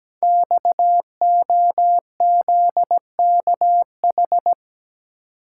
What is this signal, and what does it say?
Morse code, heard as a keyed tone, about 17 wpm, 700 Hz. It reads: XOZKH
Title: Code Groups - 5 Letter:Numbers - 17wpm